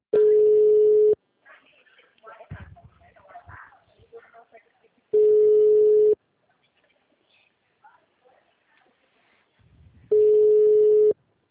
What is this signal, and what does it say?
Ringback tone (Europe, 425 Hz), with speech.
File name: out- -2016- - - .41977